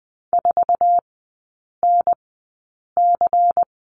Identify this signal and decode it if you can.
Morse code, heard as a keyed tone, about 20 wpm, spaced as for 8 wpm, 700 Hz. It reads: 4NC
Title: 093 Lesson 16e - Letter c - Three Characters 20wpm